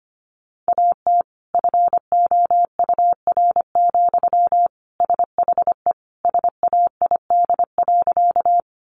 Morse code, heard as a keyed tone, about 25 wpm, 700 Hz. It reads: AT FOUR, SHE SAID.
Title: Sentences from Top 300 Words 25wpm